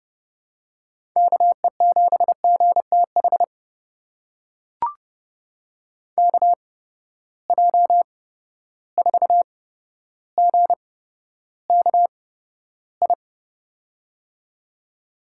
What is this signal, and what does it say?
Morse code, heard as a keyed tone, about 31 wpm, spaced as for 13 wpm, 700 Hz. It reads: KE7GTH KJ4GKI
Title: USA Call Signs - 8x Character Spacing - 30wpm